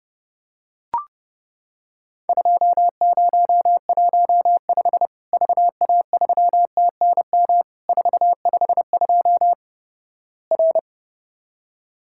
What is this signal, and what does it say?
Morse code, heard as a keyed tone, about 28 wpm, 700 Hz. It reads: 2015 VA3TNM 452 R
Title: Worked All Europe DX Contest - QTC Exchange - 30wpm